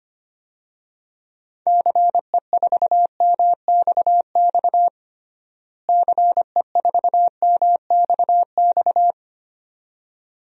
Morse code, heard as a keyed tone, about 25 wpm, 700 Hz. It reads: CE4MXX CE4MXX
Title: DXCC-100 Call Signs - 6 Characters - Repeated 2x - 25wpm